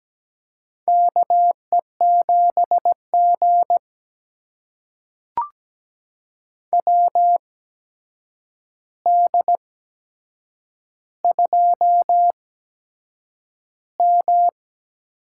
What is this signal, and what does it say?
Morse code, heard as a keyed tone, about 17 wpm, spaced as for 7 wpm, 700 Hz. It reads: KE7G WD2M
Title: USA Call Signs - 4 Characters - 8x Character Spacing - 17wpm